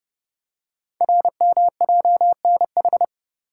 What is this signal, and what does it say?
Morse code, heard as a keyed tone, about 30 wpm, 700 Hz. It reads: RMJNH
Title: Code Groups - 5 Letters - 30wpm